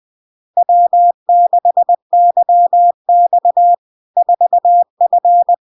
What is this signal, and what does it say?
Morse code, heard as a keyed tone, about 20 wpm, 700 Hz. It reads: W6YX 4F
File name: LARK-FD-2022-Log-in-CW-20wpm-700-Hz